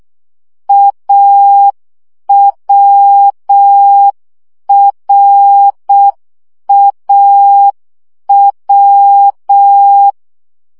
Morse code, heard as a keyed tone, about 6 wpm, 800 Hz. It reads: AWRAW
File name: CW gyakorlas a r w 5 baud